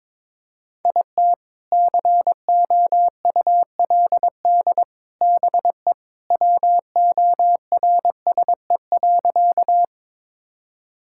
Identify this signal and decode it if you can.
Morse code, heard as a keyed tone, about 22 wpm, 700 Hz. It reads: IT COULD BE WORSE.
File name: practice / 22wpm / Sentences with 3-Word Phrases - 22wpm